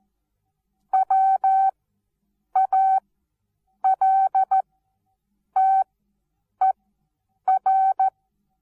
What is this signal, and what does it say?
Morse code, heard as a keyed tone, about 14 wpm, spaced as for 8 wpm, 770 Hz. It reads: WALTER
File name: Onlinelehrgang / morsen / DJ4UF Morsen19